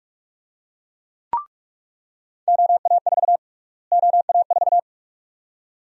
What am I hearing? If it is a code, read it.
Morse code, heard as a keyed tone, about 46 wpm, 700 Hz. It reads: OA4 OA4
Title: USA Call Sign Prefixes - Repeated 2x - 45wpm